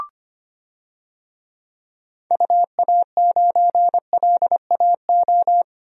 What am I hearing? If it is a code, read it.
Morse code, heard as a keyed tone, about 25 wpm, 700 Hz. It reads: UA9LAO
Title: DXCC-100 Call Signs - 6 Characters - Repeated 2x - 25wpm